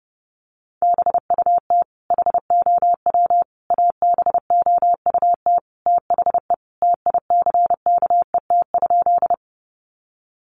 Morse code, heard as a keyed tone, about 30 wpm, 700 Hz. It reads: BUT HOW ABOUT THE TICKET?
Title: Sentences from Top 1400 Words 30wpm